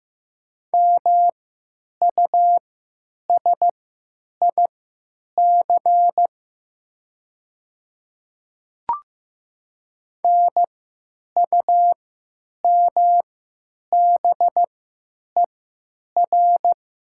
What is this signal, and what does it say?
Morse code, heard as a keyed tone, about 15 wpm, spaced as for 9 wpm, 700 Hz. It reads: MUSIC NUMBER
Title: 3x Character Spacing - Top 300 Words - 15wpm